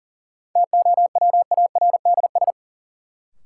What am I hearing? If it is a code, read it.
Morse code, heard as a keyed tone, about 40 wpm, 700 Hz. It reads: TOWARDS
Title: Sentences from Top 1000 Words - Review Part 2 - New Word Drill - 40wpm